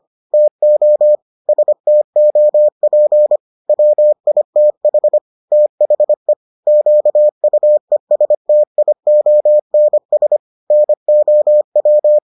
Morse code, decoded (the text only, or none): TO STOP WITH THE QUESTIONS NOW